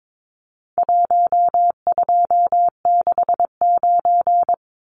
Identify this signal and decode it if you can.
Morse code, heard as a keyed tone, about 22 wpm, 700 Hz. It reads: 1269
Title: Numbers - Contest Style - 4-Digit 22wpm